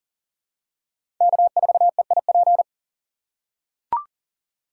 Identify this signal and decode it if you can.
Morse code, heard as a keyed tone, about 40 wpm, 690 Hz. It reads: K4EIP
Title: USA Call Signs - 5 Characters - 6x Character Spacing - 40wpm